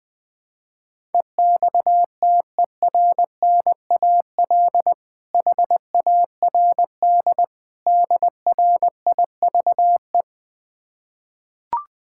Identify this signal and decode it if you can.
Morse code, heard as a keyed tone, about 20 wpm, 700 Hz. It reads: EXTERNAL HARD DRIVE
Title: Difficult Words with Hints - 20wpm